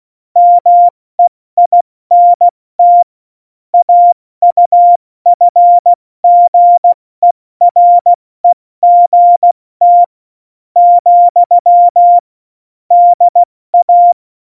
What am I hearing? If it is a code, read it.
Morse code, heard as a keyed tone, about 16 wpm, spaced as for 14 wpm, 700 Hz. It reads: MEINT AUFGEREGT, DA
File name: Morsekurs / Witze / Witz 01. Oktober 2019, 70 F80